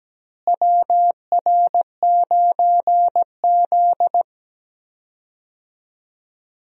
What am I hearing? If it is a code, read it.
Morse code, heard as a keyed tone, about 17 wpm, 700 Hz. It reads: WR9Z
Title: USA Call Signs - 4 Characters - Repeated 2x - 17wpm